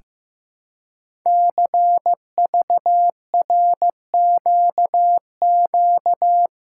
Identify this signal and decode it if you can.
Morse code, heard as a keyed tone, about 15 wpm, 700 Hz. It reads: CVRQQ
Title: Code Groups - 5 Letters - 15wpm